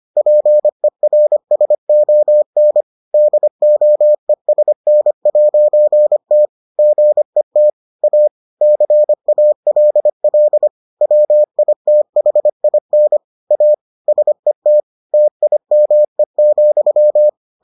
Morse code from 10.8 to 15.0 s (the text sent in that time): WITHIN A SET